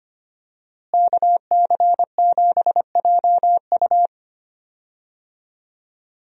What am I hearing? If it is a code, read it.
Morse code, heard as a keyed tone, about 25 wpm, 700 Hz. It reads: KC7JU
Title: USA Call Signs - 5 Characters - Repeated 2x - 25wpm